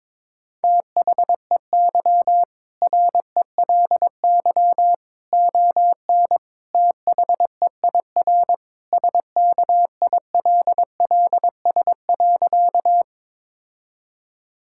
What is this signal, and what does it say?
Morse code, heard as a keyed tone, about 22 wpm, 700 Hz. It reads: THEY RELY ON THEIR SKILLS.
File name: practice / 22wpm / Sentences with 2-Word Phrases - 22wpm